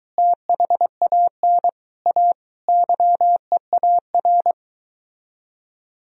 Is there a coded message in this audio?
Yes: Morse code from 0.2 to 4.5 s, heard as a keyed tone, about 23 wpm, 700 Hz. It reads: THAN A YEAR